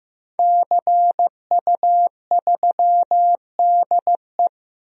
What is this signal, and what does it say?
Morse code, heard as a keyed tone, about 15 wpm, 700 Hz. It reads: CU3DE
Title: DXCC-100 Call Signs - 5 Characters - Repeated 3x - 15wpm